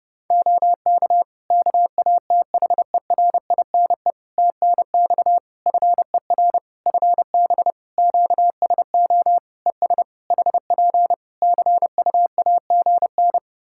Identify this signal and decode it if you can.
Morse code, heard as a keyed tone, about 30 wpm, 700 Hz. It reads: OK KATHERINE TNX FER FB QSO ES HP CUAGN
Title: K1USN SST QSO Practice 30wpm